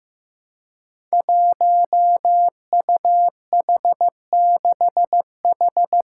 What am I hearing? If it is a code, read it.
Morse code, heard as a keyed tone, about 15 wpm, 700 Hz. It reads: 1UH6H
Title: Five Challenging Letter-Number Combinations 15wpm